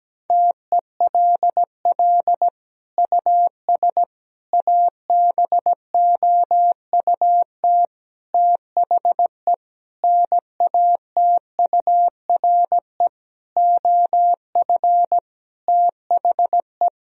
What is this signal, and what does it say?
Morse code, heard as a keyed tone, about 17 wpm, 700 Hz. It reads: TELL US ABOUT THE NATURE OF THE
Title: Sentences from Top 1000 Words - Review Part 4 - 17wpm